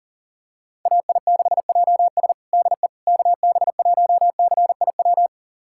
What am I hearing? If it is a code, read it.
Morse code, heard as a keyed tone, about 40 wpm, 700 Hz. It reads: AI6JS DE KB1CIW
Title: USA Callsign DE Callsign 40wpm